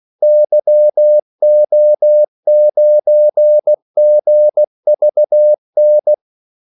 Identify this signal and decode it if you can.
Morse code, heard as a keyed tone, about 16 wpm, 600 Hz. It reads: YO9GVN
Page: Kaldesignaler 16 wpm